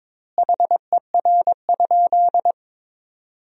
Morse code, heard as a keyed tone, about 22 wpm, 700 Hz. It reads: HER?
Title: 2x Word-Spacing - Sentences from Top 200 Words - 22wpm